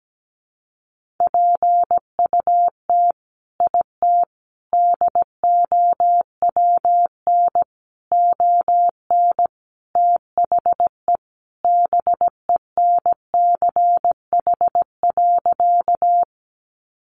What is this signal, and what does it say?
Morse code, heard as a keyed tone, about 17 wpm, 700 Hz. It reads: PUT IT DOWN ON THE BENCH.
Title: Sentences from Top 1400 Words 17wpm